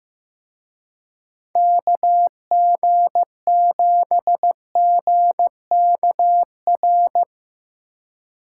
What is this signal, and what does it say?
Morse code, heard as a keyed tone, about 16 wpm, 700 Hz. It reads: KG7GKR
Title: Speed Racing - USA 2x3 Call Signs 15wpm